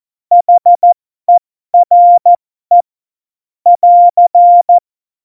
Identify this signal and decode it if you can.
Morse code, heard as a keyed tone, about 14 wpm, spaced as for 12 wpm, 700 Hz. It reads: HERE <AR>
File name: Morsekurs / Witze